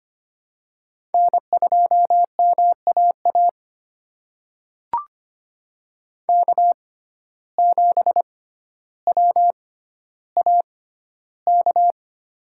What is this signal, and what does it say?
Morse code, heard as a keyed tone, about 25 wpm, spaced as for 13 wpm, 700 Hz. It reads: N2MAA K7WAK
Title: USA Call Signs - 5 Characters - 6x Character Spacing - 25wpm